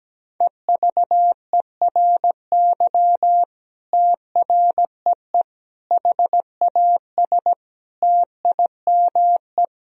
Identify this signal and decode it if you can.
Morse code, heard as a keyed tone, about 17 wpm, 700 Hz. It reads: EVERY TREE HAS TIME
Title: Sentences from Top 1000 Words - Review Part 1 - 17wpm